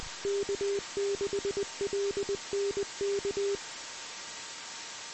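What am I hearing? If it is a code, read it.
Morse code, heard as a keyed tone, about 20 wpm, 400 Hz, with noise. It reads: K6LNK
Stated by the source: Morse signal recorded on the 2 m band at 146.115 MHz.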